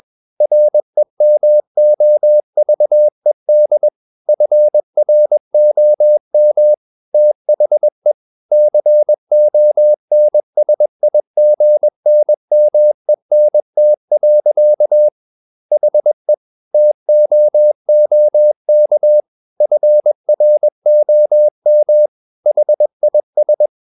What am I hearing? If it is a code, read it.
Morse code, heard as a keyed tone, about 21 wpm, 600 Hz. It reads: REMOVED FROM THE CONSIGNMENT. HE TOOK FROM HIS
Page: CW med Gnister